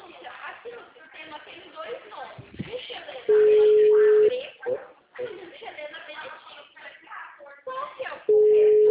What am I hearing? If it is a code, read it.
Ringback tone (Europe, 425 Hz), with speech.